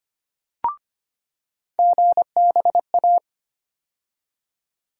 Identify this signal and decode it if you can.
Morse code, heard as a keyed tone, about 25 wpm, 700 Hz. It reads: GBA
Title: USA Call Sign Postfixes - 3 Letters 25wpm